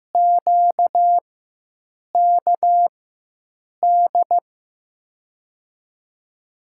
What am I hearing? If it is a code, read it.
Morse code, heard as a keyed tone, about 15 wpm, spaced as for 7 wpm, 700 Hz. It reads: QKD
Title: USA Call Signs - 6 Characters - 4x Character Spacing - 15wpm